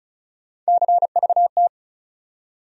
Morse code, heard as a keyed tone, about 34 wpm, 700 Hz. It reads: CVT
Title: USA Call Sign Postfixes - 3 Letters 35wpm